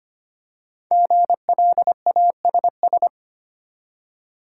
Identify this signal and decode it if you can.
Morse code, heard as a keyed tone, about 25 wpm, 700 Hz. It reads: GLASS
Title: Five Letter Words - 25wpm